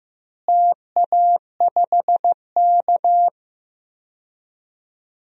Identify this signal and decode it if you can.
Morse code, heard as a keyed tone, about 15 wpm, 700 Hz. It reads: TA5K
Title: DXCC-100 Call Signs - 4 Characters - Repeated 2x - 15wpm